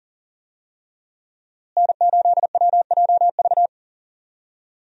Morse code, heard as a keyed tone, about 40 wpm, 700 Hz. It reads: N8WJV
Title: USA Call Signs - 5 Characters - Repeated 3x - 40wpm